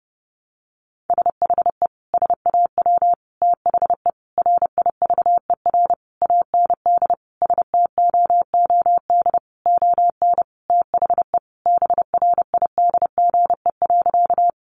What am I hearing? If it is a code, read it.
Morse code, heard as a keyed tone, about 30 wpm, 700 Hz. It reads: SHE SAW THE RIVER AND STOOD ON THE BRIDGE.
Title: Sentences from Top 1200 Words 30wpm